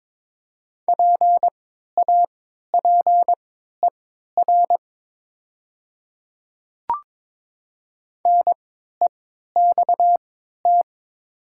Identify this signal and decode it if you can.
Morse code, heard as a keyed tone, about 22 wpm, spaced as for 12 wpm, 700 Hz. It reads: PAPER NEXT
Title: 3x Character Spacing - Top 300 Words - 22wpm